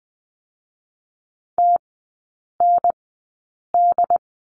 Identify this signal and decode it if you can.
Morse code, heard as a keyed tone, about 20 wpm, spaced as for 8 wpm, 700 Hz. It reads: TND
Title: 067 Lesson 12e - Letter d - Three Characters 20wpm